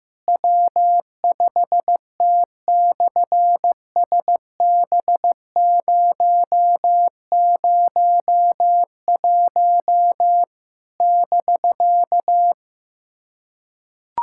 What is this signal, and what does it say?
Morse code, heard as a keyed tone, about 15 wpm, 700 Hz. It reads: W5T/SB001 <BK>
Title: SOTA QSO Practice 15wpm